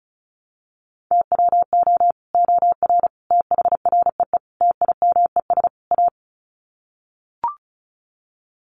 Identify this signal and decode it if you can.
Morse code, heard as a keyed tone, about 34 wpm, 700 Hz. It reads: TWO OR THREE TIMES A
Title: Sets of 5 Words Encore2 35wpm